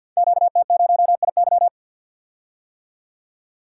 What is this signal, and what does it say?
Morse code, heard as a keyed tone, about 53 wpm, 700 Hz. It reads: QT0IY
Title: Code Groups - 5 Letter:Numbers - 50wpm